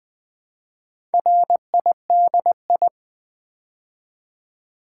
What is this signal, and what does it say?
Morse code, heard as a keyed tone, about 20 wpm, 700 Hz. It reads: RIDI
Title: Build A Word - 4 Letters At A Time - 10-Letter Words - 20wpm